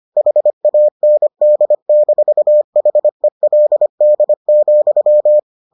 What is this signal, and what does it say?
Morse code, heard as a keyed tone, about 25 wpm, 600 Hz. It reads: HAND-HELD,